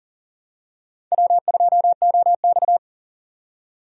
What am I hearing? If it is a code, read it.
Morse code, heard as a keyed tone, about 40 wpm, 690 Hz. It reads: W2OX
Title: USA Call Signs - 4 Characters - 40wpm